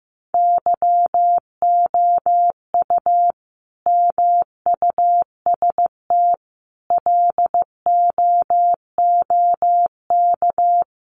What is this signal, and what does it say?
Morse code, heard as a keyed tone, about 15 wpm, 700 Hz. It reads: YOU MUST LOOK
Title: Sentences from Top 1100 Words 15wpm